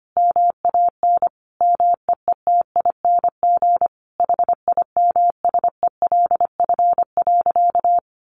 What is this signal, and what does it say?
Morse code, heard as a keyed tone, about 25 wpm, 700 Hz. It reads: MAN MEETING HIMSELF.
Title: A Sound Of Thunder 25wpm